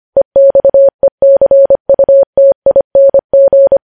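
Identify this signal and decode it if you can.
Morse code, heard as a keyed tone, about 25 wpm, 550 Hz. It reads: EXECUTING
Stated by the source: Morse code news headlines sent at 25 wpm.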